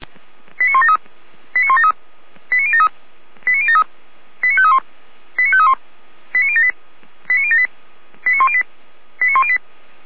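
Five-tone selective call (ZVEI), heard as quick runs of five tones, tones of about 70 ms. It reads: 89173 89173 89073 89073 89531 89531 89078 89078 89108 89108
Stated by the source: ZVEI five-tone selective call received on an SDR.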